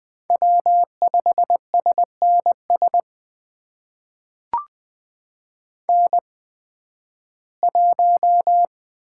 Morse code, heard as a keyed tone, about 20 wpm, spaced as for 13 wpm, 700 Hz. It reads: W5SNS N1
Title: USA Call Signs - 8x Character Spacing - 20wpm